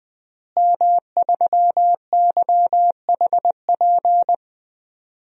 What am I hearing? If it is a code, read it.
Morse code, heard as a keyed tone, about 20 wpm, 700 Hz. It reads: M3YHP
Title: DXCC-100 Call Signs - 5 Characters - Repeated 2x - 20wpm